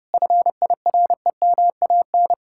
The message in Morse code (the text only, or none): FIREMAN